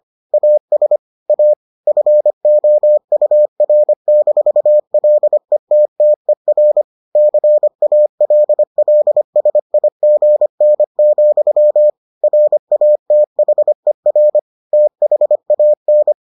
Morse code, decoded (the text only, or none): AS A FOUR-LETTER CALLSIGN, RATHER THAN